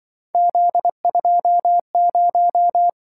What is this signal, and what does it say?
Morse code, heard as a keyed tone, about 24 wpm, 700 Hz. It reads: Z20
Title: Speed-Racing - Three Letter-Number Combinations - 20wpm